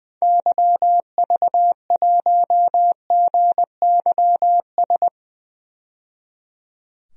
Morse code, heard as a keyed tone, about 20 wpm, 700 Hz. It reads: YV1GYS
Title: DXCC-100 Call Signs - 6 Characters - 20wpm